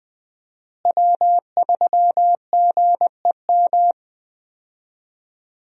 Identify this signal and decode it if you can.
Morse code, heard as a keyed tone, about 21 wpm, 700 Hz. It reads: W3GEM
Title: USA Call Signs - 5 Characters - Repeated 2x - 20wpm